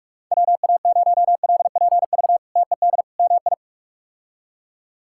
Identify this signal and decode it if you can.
Morse code, heard as a keyed tone, about 42 wpm, 700 Hz. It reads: WA0LPV TED MI
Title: K1USN SST QSO Practice - 45wpm